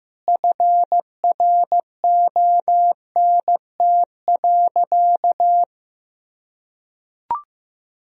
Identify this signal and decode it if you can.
Morse code, heard as a keyed tone, about 15 wpm, 700 Hz. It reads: FRONT.
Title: Sentences from Top 1000 Words - Review Part 2 - 15wpm